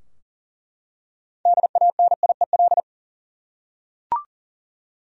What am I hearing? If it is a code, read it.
Morse code, heard as a keyed tone, about 40 wpm, 700 Hz. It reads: DANIEL
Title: USA Common Names - 40wpm